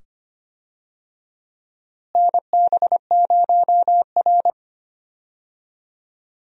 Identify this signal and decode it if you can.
Morse code, heard as a keyed tone, about 25 wpm, 700 Hz. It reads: NB0R
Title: USA Call Signs - 4 Characters - Repeated 2x - 25wpm